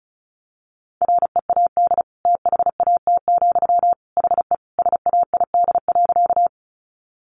Morse code, heard as a keyed tone, about 34 wpm, 700 Hz. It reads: READ THAT, HE SAID.
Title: Sentences from Top 200 Words 35wpm